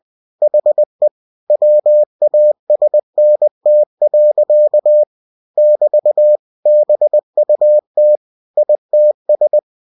Morse code, heard as a keyed tone, about 20 wpm, 600 Hz. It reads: HE WASNT. <BT> BUT ITS
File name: cw-books / winnie-the-pooh / winnie-20wpm-page-0099